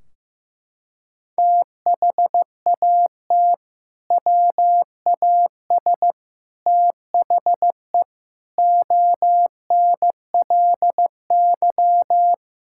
Morse code, heard as a keyed tone, about 15 wpm, 700 Hz. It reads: THAT WAS THE ONLY
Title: Sentences from Top 1600 Words 15wpm